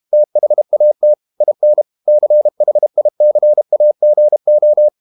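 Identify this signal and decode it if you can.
Morse code, heard as a keyed tone, about 32 wpm, 600 Hz. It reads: THAT IN CHICAGO
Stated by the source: Morse code practice at 32 wpm from CW med Gnister, reading Ken Follett's novel Never in English.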